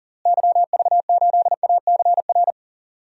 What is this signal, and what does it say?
Morse code, heard as a keyed tone, about 37 wpm, 700 Hz. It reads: YV8ACR
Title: DXCC-100 Call Signs - 6 Characters - Repeated 3x - 40wpm